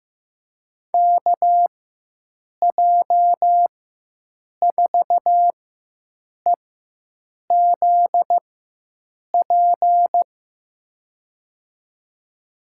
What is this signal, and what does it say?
Morse code, heard as a keyed tone, about 15 wpm, spaced as for 7 wpm, 700 Hz. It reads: KJ4EZP
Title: USA Call Signs - 6 Characters - 4x Character Spacing - 15wpm